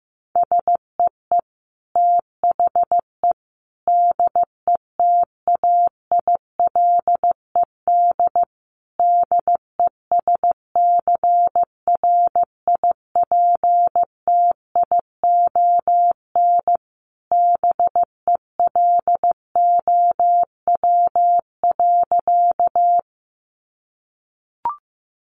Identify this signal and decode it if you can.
Morse code, heard as a keyed tone, about 15 wpm, 700 Hz. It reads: SEE THE DETAILED DESCRIPTION BELOW.
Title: Sentences from Top 1500 Words 15wpm